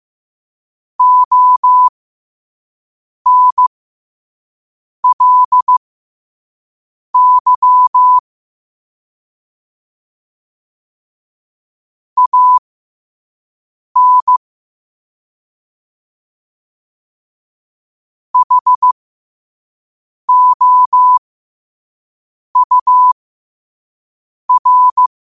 Morse code, heard as a keyed tone, about 15 wpm, spaced as for 5 wpm, 1000 Hz. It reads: ONLY AN HOUR
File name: qotdfiles / 5wpm